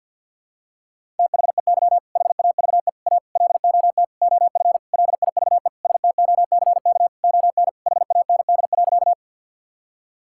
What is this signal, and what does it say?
Morse code, heard as a keyed tone, about 50 wpm, 700 Hz. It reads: THEY HAVE A LOT OF LIVE STOCK ON HAND.
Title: Sentences from Top 1000 Words - Review Part 5 - 50wpm